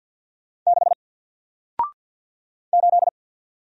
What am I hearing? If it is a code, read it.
Morse code, heard as a keyed tone, about 48 wpm, 700 Hz. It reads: 6 8
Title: Single Letter-Number Alt 50wpm